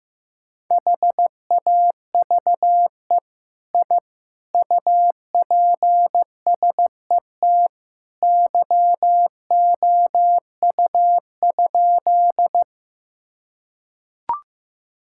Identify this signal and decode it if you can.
Morse code, heard as a keyed tone, about 14 wpm, 700 Hz. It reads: HAVE I UPSET YOU?
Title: Sentences from Top 1000 Words - Review Part 5 - 15wpm